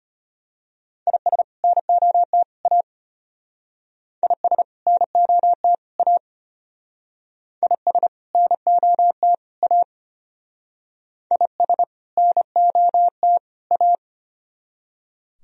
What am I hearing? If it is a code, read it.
Morse code, heard as a keyed tone, about 33 wpm, 700 Hz. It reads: IS NOT A IS NOT A IS NOT A IS NOT A